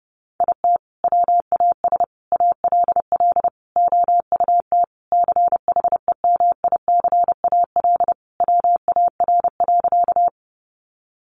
Morse code, heard as a keyed tone, about 30 wpm, 700 Hz. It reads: IT WAS ALL OUT CHEMICAL WAR.
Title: Sentences from Top 1200 Words 30wpm